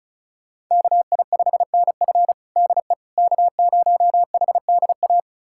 Morse code, heard as a keyed tone, about 36 wpm, 700 Hz. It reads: KI5NF DE K0HDA
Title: USA Callsign DE Callsign - 5 Characters - 35wpm